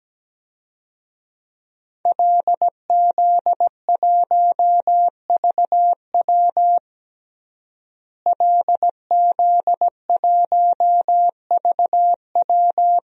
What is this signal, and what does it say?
Morse code, heard as a keyed tone, about 17 wpm, 700 Hz. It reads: LZ1VW LZ1VW
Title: DXCC-100 Call Signs - 5 Characters - Repeated 3x - 17wpm